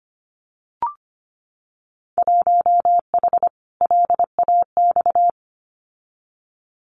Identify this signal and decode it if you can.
Morse code, heard as a keyed tone, about 25 wpm, 700 Hz. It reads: 1H LAX
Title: Winter Field Day Exchanges 25wpm